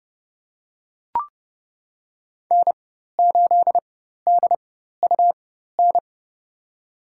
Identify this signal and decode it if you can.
Morse code, heard as a keyed tone, about 30 wpm, spaced as for 14 wpm, 700 Hz. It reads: N8DUN
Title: USA Call Signs - 4x Character Spacing - 30wpm